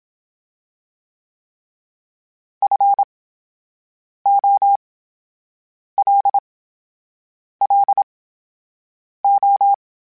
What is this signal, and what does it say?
Morse code, heard as a keyed tone, about 27 wpm, spaced as for 7 wpm, 800 Hz. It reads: FOLLO
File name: Special INGs / Special INGs-mp3s / VerySlowToFaster Ing @25